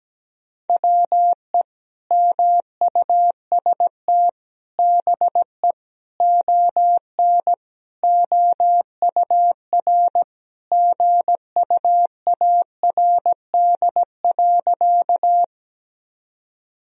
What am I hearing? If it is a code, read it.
Morse code, heard as a keyed tone, about 17 wpm, 700 Hz. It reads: WE MUST BE ON OUR GUARD.
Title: Sentences from Top 1000 Words - Review Part 4 - 17wpm